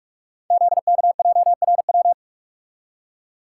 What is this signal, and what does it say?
Morse code, heard as a keyed tone, about 46 wpm, 700 Hz. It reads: ZKJRW
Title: Code Groups - 5 Letter:Numbers - 45wpm